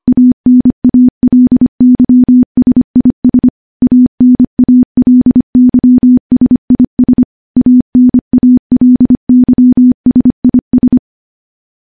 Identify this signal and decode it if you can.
Morse code, heard as a keyed tone, about 25 wpm, 250 Hz. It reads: ANALYSIS ANALYSIS ANALYSIS
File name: morse / morse-training-25WPM